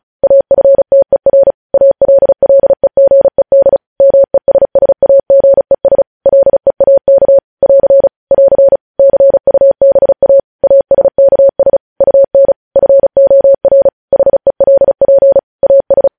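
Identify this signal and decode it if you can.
Morse code, heard as a keyed tone, about 36 wpm, 550 Hz. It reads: AFTER ALLEGED MESSAGES LEAK <AR> <AR> CUBA ASKS UN FOR HELP AS
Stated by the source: News headlines sent in Morse at speed 35.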